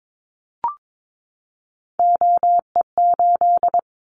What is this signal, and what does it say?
Morse code, heard as a keyed tone, about 22 wpm, 700 Hz. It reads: OE8
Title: Callsign Prefixes for Top 25 DXCC - 22wpm